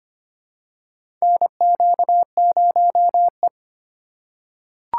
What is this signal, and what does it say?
Morse code, heard as a keyed tone, about 25 wpm, 700 Hz. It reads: NQ0E
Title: USA Call Signs - 4 Characters - 3x Character Spacing - 25wpm